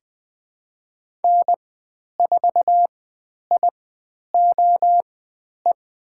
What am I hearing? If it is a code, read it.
Morse code, heard as a keyed tone, about 20 wpm, spaced as for 10 wpm, 700 Hz. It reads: N4IOE
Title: 095 Lesson 16g - Letter c - Call Signs 20wpm